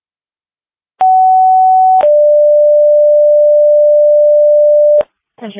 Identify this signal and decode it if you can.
Two-tone page, heard as a one-second tone, then a three-second tone, about 750 then 590 Hz, with speech. It is heard from 1.0 to 5.0 s.